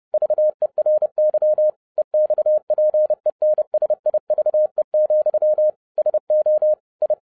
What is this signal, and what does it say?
Morse code, heard as a keyed tone, about 30 wpm, 610 Hz. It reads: VERY EXPENSIVE, SO I